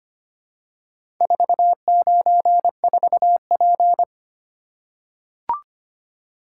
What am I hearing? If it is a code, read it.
Morse code, heard as a keyed tone, about 25 wpm, 700 Hz. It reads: 494P
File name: practice / 25wpm / Four Letter-Number Combinations - 25wpm